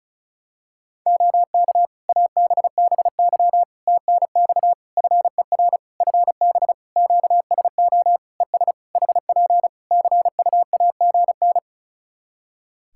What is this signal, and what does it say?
Morse code, heard as a keyed tone, about 34 wpm, 700 Hz. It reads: OK ABBY TNX FER FB QSO ES HP CUAGN
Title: Practice QSO - Exchange 4 - Ending - 35wpm